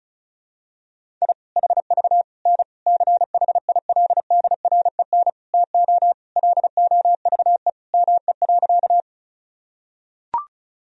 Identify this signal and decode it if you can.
Morse code, heard as a keyed tone, about 34 wpm, 700 Hz. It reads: I HV N CHILDREN TO LOVE ME.
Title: Sentences from Top 400 Words and Abbreviations 35wpm